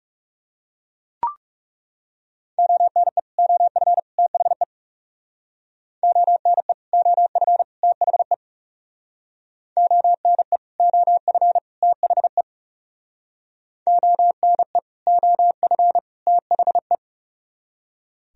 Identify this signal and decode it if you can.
Morse code, heard as a keyed tone, about 36 wpm, 700 Hz. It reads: ONE OF THE ONE OF THE ONE OF THE ONE OF THE
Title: Speed-Racing - Sets of 3 Words 30wpm